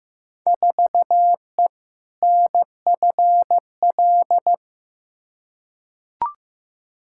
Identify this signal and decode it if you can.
Morse code, heard as a keyed tone, about 15 wpm, 700 Hz. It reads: 4E NFL
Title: Field Day Exchanges Encore - 15wpm